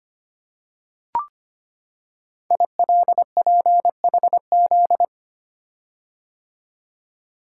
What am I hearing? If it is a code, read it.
Morse code, heard as a keyed tone, about 25 wpm, 700 Hz. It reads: ILPHZ
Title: Code Groups - 5 Letter:Numbers - 25wpm